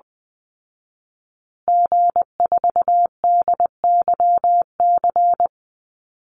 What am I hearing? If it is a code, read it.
Morse code, heard as a keyed tone, about 20 wpm, 700 Hz. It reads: G4DYC
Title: Call Signs Repeated 3x - 20wpm